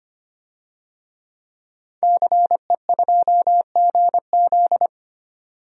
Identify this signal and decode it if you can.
Morse code, heard as a keyed tone, about 25 wpm, 700 Hz. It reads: CE2GZ
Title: DXCC-100 Call Signs - 5 Characters - 25wpm